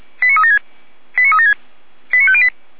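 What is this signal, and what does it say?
Five-tone selective call (ZVEI), heard as quick runs of five tones, tones of about 70 ms. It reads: 89376 89376 89508
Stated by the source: ZVEI five-tone selective call received on an SDR.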